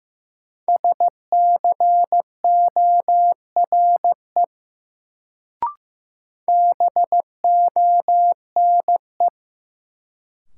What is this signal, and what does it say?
Morse code, heard as a keyed tone, about 15 wpm, 700 Hz. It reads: SCORE BONE
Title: Sentences from Top 1600 Words - New Word Drill Alt - 15wpm